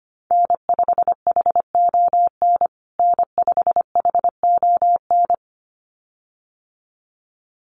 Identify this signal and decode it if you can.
Morse code, heard as a keyed tone, about 25 wpm, 700 Hz. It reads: N5HON N5HON
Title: USA Call Signs Repeated 2x 25wpm